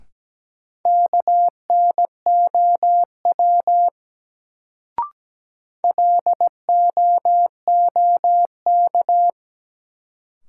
Morse code, heard as a keyed tone, about 17 wpm, 700 Hz. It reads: KNOW LOOK
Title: Sentences from Top 100 Words - New Word Drill - 17wpm